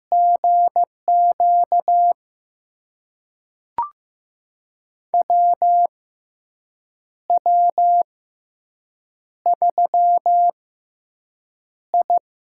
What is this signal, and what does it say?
Morse code, heard as a keyed tone, about 15 wpm, spaced as for 6 wpm, 700 Hz. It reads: GQ WW3I
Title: USA Call Signs - 4 Characters - 6x Character Spacing - 15wpm